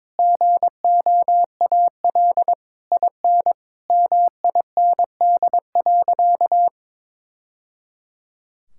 Morse code, heard as a keyed tone, about 22 wpm, 700 Hz. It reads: GOAL IN MIND.
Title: Sentences from Top 1000 Words - Review Part 5 - 22wpm